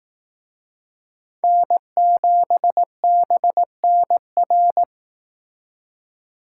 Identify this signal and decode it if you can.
Morse code, heard as a keyed tone, about 18 wpm, 700 Hz. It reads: N7BNR
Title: Speed Racing - USA 1x3 Call Signs 15wpm